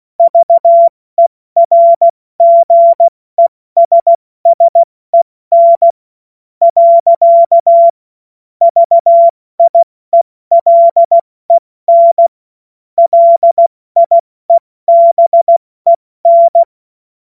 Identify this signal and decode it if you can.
Morse code, heard as a keyed tone, about 16 wpm, spaced as for 14 wpm, 680 Hz. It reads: VERGESSEN. VIELEN LIEBEN
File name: Morsekurs / Witze / Witz 04. April 70 F80